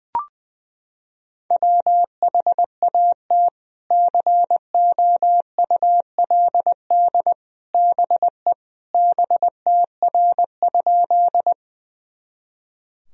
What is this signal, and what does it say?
Morse code, heard as a keyed tone, about 20 wpm, 700 Hz. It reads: WHAT COULD BE BTR?
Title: Sentences from Top 400 Words and Abbreviations 20wpm